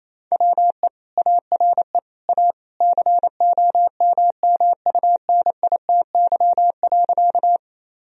Morse code, heard as a keyed tone, about 28 wpm, 700 Hz. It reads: WE ARE A COMMUNITY.
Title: Sentences from Top 1000 Words - Review Part 4 - 28wpm